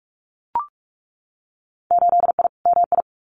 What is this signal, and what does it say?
Morse code, heard as a keyed tone, about 44 wpm, 700 Hz. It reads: 8I MI
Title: Winter Field Day Exchanges Encore 45wpm